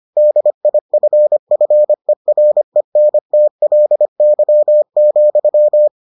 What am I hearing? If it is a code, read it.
Morse code, heard as a keyed tone, about 25 wpm, 600 Hz. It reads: DIFFERENTLY,